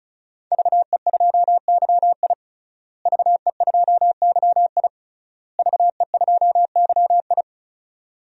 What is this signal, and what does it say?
Morse code, heard as a keyed tone, about 36 wpm, 700 Hz. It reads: VE2YI VE2YI VE2YI
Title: DXCC-100 Call Signs - 5 Characters - Repeated 3x - 35wpm